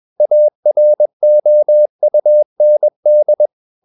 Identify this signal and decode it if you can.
Morse code, heard as a keyed tone, about 21 wpm, 600 Hz. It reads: AROUND
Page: CW med Gnister